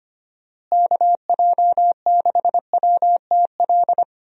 Morse code, heard as a keyed tone, about 25 wpm, 700 Hz. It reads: KJ6WTL
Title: USA Call Signs - 6 Characters - Repeated 3x - 25wpm